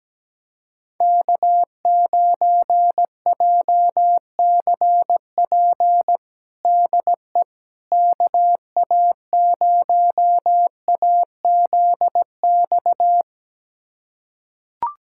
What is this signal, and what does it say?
Morse code, heard as a keyed tone, about 17 wpm, 700 Hz. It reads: K9JCP DE KA0AZX
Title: USA Callsign DE Callsign 17wpm